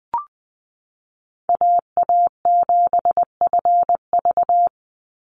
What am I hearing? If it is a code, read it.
Morse code, heard as a keyed tone, about 20 wpm, 700 Hz. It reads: AA7FV
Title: Call Signs Repeated 3x - 20wpm